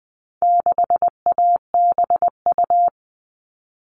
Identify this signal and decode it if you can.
Morse code, heard as a keyed tone, about 20 wpm, 700 Hz. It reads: 6ABU
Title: All Cut Numbers - 4-Digit 20wpm